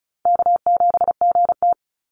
Morse code, heard as a keyed tone, about 34 wpm, 700 Hz. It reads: K7GT
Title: Easy Call Signs Repeated 2x - 35wpm